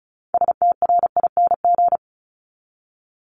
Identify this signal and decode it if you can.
Morse code, heard as a keyed tone, about 36 wpm, 700 Hz. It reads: STRING
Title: Multi-letter Phonemes 35wpm